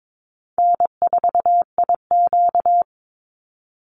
Morse code, heard as a keyed tone, about 22 wpm, 700 Hz. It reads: N4IQ
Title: Easy Call Signs - 22wpm